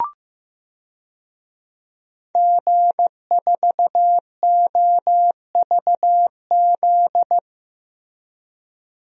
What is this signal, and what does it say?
Morse code, heard as a keyed tone, about 16 wpm, 700 Hz. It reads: G4OVZ
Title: DXCC-100 Call Signs - 5 Characters - 15wpm